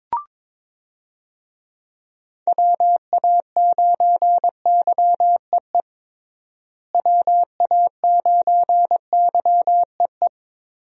Morse code, heard as a keyed tone, about 22 wpm, 700 Hz. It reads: WA9YEE WA9YEE
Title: USA Call Signs - 6 Characters - Repeated 3x - 22wpm